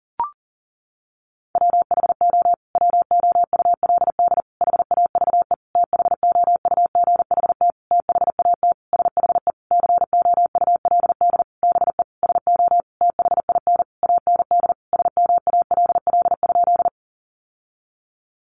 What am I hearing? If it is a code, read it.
Morse code, heard as a keyed tone, about 40 wpm, 700 Hz. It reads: WHO WOULD HAVE THOUGHT THAT SHE COULD BE SO THIN AND SMALL?
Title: Sentences from Top 1200 Words 40wpm